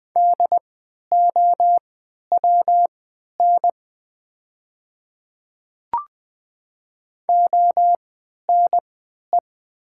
Morse code, heard as a keyed tone, about 20 wpm, spaced as for 11 wpm, 700 Hz. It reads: DOWN ONE
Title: 3x Character Spacing - Top 100 Words - 20wpm